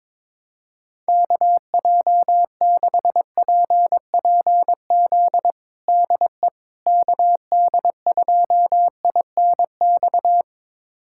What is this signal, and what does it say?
Morse code, heard as a keyed tone, about 22 wpm, 700 Hz. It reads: KJ6PPZ DE KD2INX
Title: USA Callsign DE Callsign 22wpm